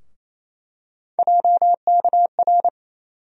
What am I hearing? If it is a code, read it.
Morse code, heard as a keyed tone, about 28 wpm, 700 Hz. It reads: JKR
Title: USA Call Sign Postfixes - 3 Letters 28wpm